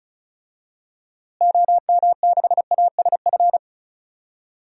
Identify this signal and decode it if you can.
Morse code, heard as a keyed tone, about 36 wpm, 700 Hz. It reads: OM6ASF
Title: DXCC-100 Call Signs - 6 Characters - Repeated 3x - 35wpm